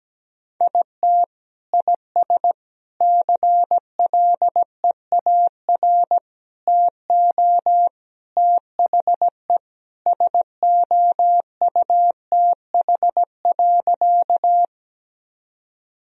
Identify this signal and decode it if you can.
Morse code, heard as a keyed tone, about 17 wpm, 700 Hz. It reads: IT IS CLEAR TO THE SOUTH.
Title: Sentences from Top 1000 Words - Review Part 2 - 17wpm